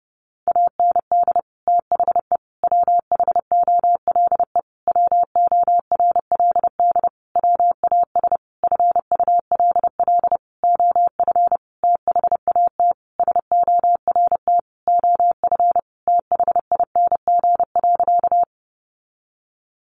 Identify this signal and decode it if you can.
Morse code, heard as a keyed tone, about 30 wpm, 700 Hz. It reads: AND THE WHOLE WORLD WAS FULL OF THAT SORT OF THING.
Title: Sentences from Top 1300 Words 30wpm